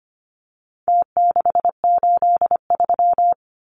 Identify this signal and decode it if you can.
Morse code, heard as a keyed tone, about 25 wpm, 700 Hz. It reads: T683
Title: Numbers - Contest Style - 4-Digit 25wpm